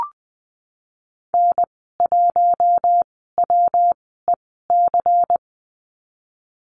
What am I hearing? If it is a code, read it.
Morse code, heard as a keyed tone, about 20 wpm, spaced as for 15 wpm, 700 Hz. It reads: N1WEC
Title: 147 Lesson 24g - Letter y - Call Signs 20wpm